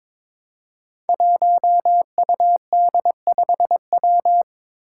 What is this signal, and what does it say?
Morse code, heard as a keyed tone, about 22 wpm, 700 Hz. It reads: 1UD5W
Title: Five Challenging Letter-Number Combinations 22wpm